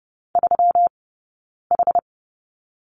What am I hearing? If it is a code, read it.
Morse code, heard as a keyed tone, about 30 wpm, 700 Hz. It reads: 3 H
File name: Lesson 20d - Number 3 - Two Characters 30wpm